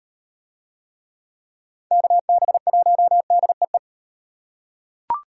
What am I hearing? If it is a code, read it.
Morse code, heard as a keyed tone, about 36 wpm, 700 Hz. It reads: KB1DEE